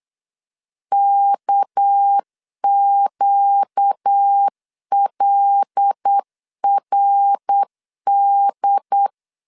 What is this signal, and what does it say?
Morse code, heard as a keyed tone, about 8 wpm, 800 Hz. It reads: KQLRD
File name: CW / DET / 1 6 1500C S B 8wpm 5142